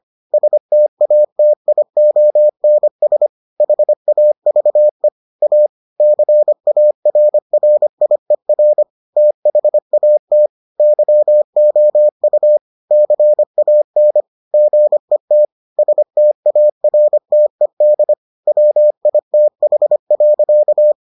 Morse code, heard as a keyed tone, about 25 wpm, 600 Hz. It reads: STATIONS HAVE A CARRIER THAT YOU CAN GET STARTED WITH.